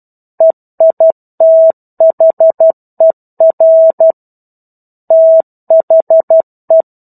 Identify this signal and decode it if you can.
Morse code, heard as a keyed tone, about 12 wpm, 650 Hz. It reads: EITHER THE
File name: 01-Chapter-01 12 WPM